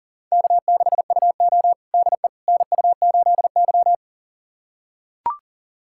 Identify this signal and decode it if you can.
Morse code, heard as a keyed tone, about 37 wpm, 690 Hz. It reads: K6UO DE NU8Y
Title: USA Callsign DE Callsign - 4 Characters - 40wpm